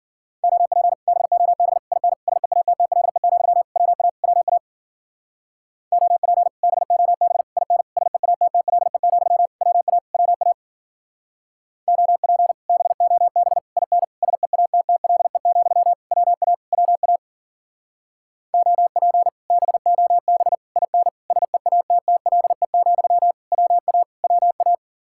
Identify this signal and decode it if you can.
Morse code, heard as a keyed tone, about 48 wpm, 700 Hz. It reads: OP BOB IN SEATTLE, WA WA OP BOB IN SEATTLE, WA WA OP BOB IN SEATTLE, WA WA OP BOB IN SEATTLE, WA WA
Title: Speed-Racing Common QSO Phrases - 40wpm